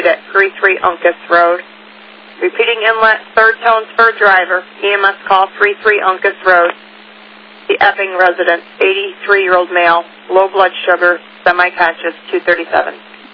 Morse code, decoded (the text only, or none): none